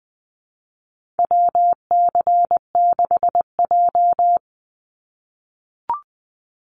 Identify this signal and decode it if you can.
Morse code, heard as a keyed tone, about 20 wpm, 700 Hz. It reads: WC6J
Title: NA SOTA Activator Callsigns 20wpm